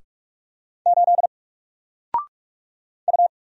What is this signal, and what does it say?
Morse code, heard as a keyed tone, about 44 wpm, 700 Hz. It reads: 8 U
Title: Single Letter-Number 45wpm